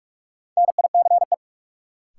Morse code, heard as a keyed tone, about 46 wpm, 700 Hz. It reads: NICE
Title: Sentences from Top 1100 Words - New Word Drill Alt - 45wpm